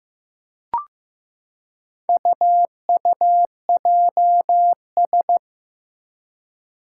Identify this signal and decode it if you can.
Morse code, heard as a keyed tone, about 15 wpm, 700 Hz. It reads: UUJS
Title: Four Challenging Letter-Number Combinations 15wpm